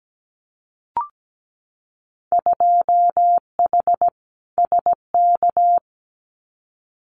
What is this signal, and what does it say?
Morse code, heard as a keyed tone, about 17 wpm, 700 Hz. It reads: 2H SK
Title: Winter Field Day Exchanges Encore 17wpm